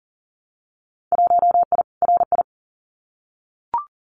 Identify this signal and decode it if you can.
Morse code, heard as a keyed tone, about 38 wpm, 700 Hz. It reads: 1I RI
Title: Winter Field Day Exchanges Encore 40wpm